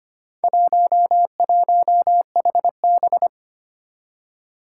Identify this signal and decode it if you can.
Morse code, heard as a keyed tone, about 25 wpm, 700 Hz. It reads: 11HB
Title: Four Challenging Letter-Number Combinations 25wpm